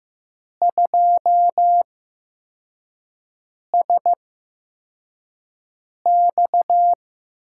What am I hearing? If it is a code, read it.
Morse code, heard as a keyed tone, about 15 wpm, spaced as for 4 wpm, 700 Hz. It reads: 2SX
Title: USA Call Signs - 4 Characters - 8x Character Spacing - 15wpm